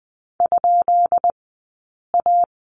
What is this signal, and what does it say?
Morse code, heard as a keyed tone, about 20 wpm, spaced as for 8 wpm, 700 Hz. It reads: ?A
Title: 177 Lesson 29d - Slash - Two Characters 20wpm